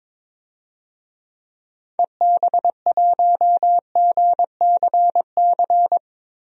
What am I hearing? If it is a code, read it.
Morse code, heard as a keyed tone, about 22 wpm, 700 Hz. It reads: EB1GCC
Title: DXCC-100 Call Signs - 6 Characters - 22wpm